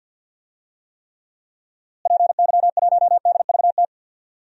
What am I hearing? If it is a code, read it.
Morse code, heard as a keyed tone, about 52 wpm, 700 Hz. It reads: PY1DVT
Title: DXCC-100 Call Signs - 6 Characters - Repeated 3x - 50wpm